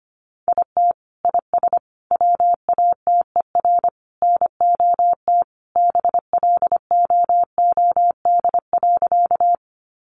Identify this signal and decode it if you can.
Morse code, heard as a keyed tone, about 25 wpm, 700 Hz. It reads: IT IS WATER NOT BLOOD.
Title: Sentences from Top 1100 Words 25wpm